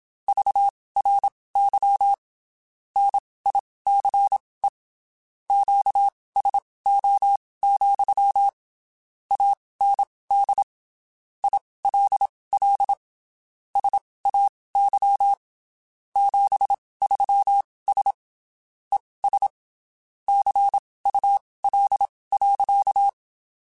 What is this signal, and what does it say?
Morse code, heard as a keyed tone, about 27 wpm, spaced as for 19 wpm, 800 Hz. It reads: VRY NICE QSO, AND ILL SAY 73S ES CUL.
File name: (spaced) / QSOs Spaced-mp3s / WA7QOX 2 @25 sp